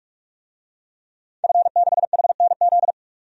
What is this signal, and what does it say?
Morse code, heard as a keyed tone, about 46 wpm, 700 Hz. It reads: F6HNZ
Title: DXCC-100 Call Signs - 5 Characters - 45wpm